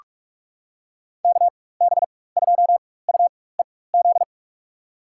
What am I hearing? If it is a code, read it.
Morse code, heard as a keyed tone, about 46 wpm, spaced as for 21 wpm, 700 Hz. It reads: KB2UEZ
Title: USA Call Signs - 6 Characters - 4x Character Spacing - 45wpm